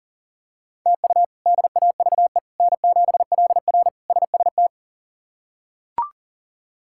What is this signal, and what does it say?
Morse code, heard as a keyed tone, about 40 wpm, 700 Hz. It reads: TU DAVE N7LR SST
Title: K1USN SST QSO Practice - 40wpm